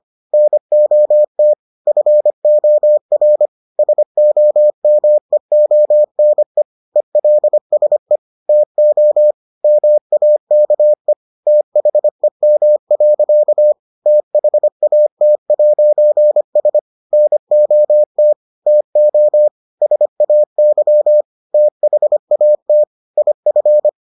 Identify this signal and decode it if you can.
Morse code, heard as a keyed tone, about 25 wpm, 600 Hz. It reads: NOT FOR SOMEONE ELSE TO MAKE THEM. THAT'S NOT TO SAY THAT IF